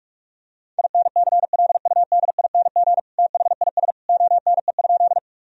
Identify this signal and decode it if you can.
Morse code, heard as a keyed tone, about 43 wpm, 700 Hz. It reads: INCLUDING THIS ONE?
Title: Sentences from Top 1000 Words - Review Part 4 - 45wpm